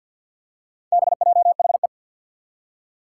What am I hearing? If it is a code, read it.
Morse code, heard as a keyed tone, about 47 wpm, 700 Hz. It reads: BJHE
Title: Four Challenging Letter-Number Combinations 50wpm